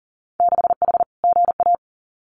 Morse code, heard as a keyed tone, about 40 wpm, 700 Hz. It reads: 6H GA
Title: Winter Field Day Exchanges Encore 40wpm